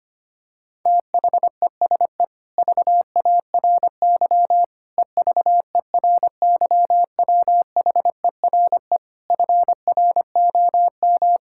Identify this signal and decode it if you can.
Morse code, heard as a keyed tone, about 25 wpm, 700 Hz. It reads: THESE VARY EVERYWHERE FROM
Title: Sentences from Top 1600 Words 25wpm